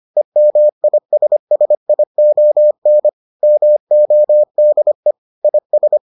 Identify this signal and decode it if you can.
Morse code, heard as a keyed tone, about 25 wpm, 600 Hz. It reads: EMISSION MODE IS